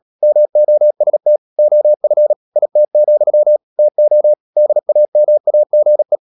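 Morse code, heard as a keyed tone, about 37 wpm, 600 Hz. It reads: MOST OF IT, TO DAMAGE
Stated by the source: Morse practice at 36 wpm, text of Ken Follett's novel Never, from CW med Gnister.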